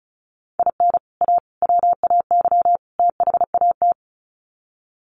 Morse code, heard as a keyed tone, about 34 wpm, 700 Hz. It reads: IN A WAY THAT
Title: Sets of 4 Words 35wpm